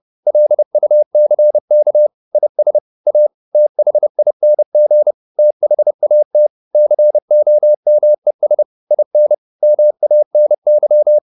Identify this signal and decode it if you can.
Morse code, heard as a keyed tone, about 30 wpm, 600 Hz. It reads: LUCK IS A THING THAT COMES IN MANY